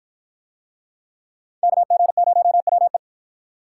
Morse code, heard as a keyed tone, about 53 wpm, 700 Hz. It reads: KG0WE